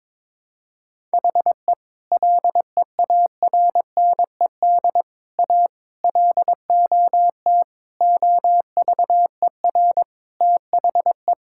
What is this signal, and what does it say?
Morse code, heard as a keyed tone, about 22 wpm, 700 Hz. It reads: HE LEARNED A LOT OVER THE
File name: practice / 22wpm / Sentences with 3-Word Phrases - 22wpm